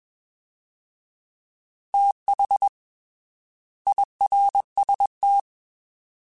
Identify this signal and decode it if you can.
Morse code, heard as a keyed tone, about 21 wpm, 800 Hz. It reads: TH IRST
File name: TH / TH mp3s / Fast TH @20